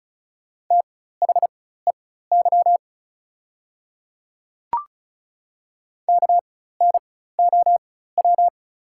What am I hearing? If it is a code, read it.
Morse code, heard as a keyed tone, about 36 wpm, spaced as for 16 wpm, 690 Hz. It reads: THEY KNOW
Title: 4x Character Spacing - Top 300 Words - 35wpm